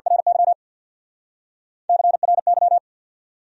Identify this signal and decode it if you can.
Morse code, heard as a keyed tone, about 65 wpm, 700 Hz. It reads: RY CRY